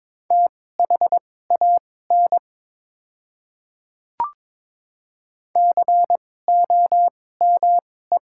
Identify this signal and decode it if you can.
Morse code, heard as a keyed tone, about 22 wpm, spaced as for 16 wpm, 700 Hz. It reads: THAN COME
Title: 2x Character Spacing - Top 100 Words - 22wpm